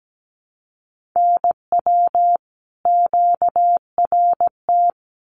Morse code, heard as a keyed tone, about 17 wpm, 700 Hz. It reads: NW QRT
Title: Common QSO Phrases 17wpm